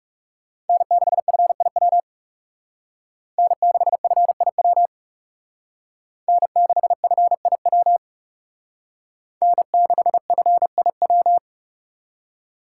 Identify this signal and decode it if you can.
Morse code, heard as a keyed tone, about 36 wpm, 700 Hz. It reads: N6FIW N6FIW N6FIW N6FIW
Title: Speed Racing - USA 1x3 Call Signs 30wpm